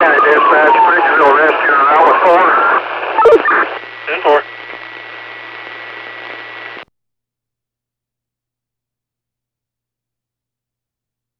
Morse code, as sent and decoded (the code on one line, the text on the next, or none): none